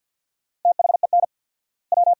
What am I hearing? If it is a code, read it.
Morse code, heard as a keyed tone, about 51 wpm, 700 Hz. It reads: THEN W
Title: 4x Word-Spacing - Sentences from Top 400 Words - 50wpm